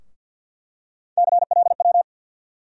Morse code, heard as a keyed tone, about 52 wpm, 700 Hz. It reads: CLW
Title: USA Call Sign Postfixes - 3 Letters 50wpm